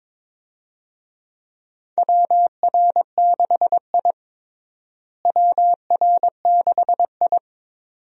Morse code, heard as a keyed tone, about 22 wpm, 700 Hz. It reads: WR6I WR6I
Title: USA Call Signs - 4 Characters - Repeated 2x - 22wpm